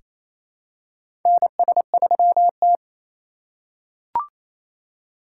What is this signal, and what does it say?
Morse code, heard as a keyed tone, about 28 wpm, 690 Hz. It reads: NS3T
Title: USA Call Signs - 2x Character Spacing - 28wpm